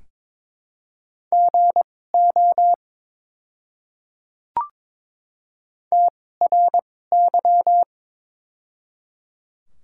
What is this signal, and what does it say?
Morse code, heard as a keyed tone, about 22 wpm, spaced as for 16 wpm, 700 Hz. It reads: GO TRY
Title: 2x Character Spacing - Top 200 Words - 22wpm